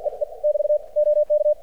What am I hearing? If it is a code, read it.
Morse code, heard as a keyed tone, about 54 wpm, 600 Hz. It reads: E E <BT> OK
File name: archived / CRSnet